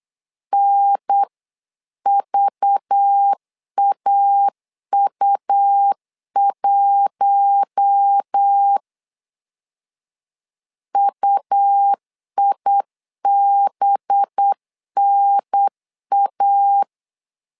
Morse code, heard as a keyed tone, about 8 wpm, 800 Hz. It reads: NVAU1 UIBNA
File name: LEZIONI / 5 LBVC / 5 LBV 8wpm 2837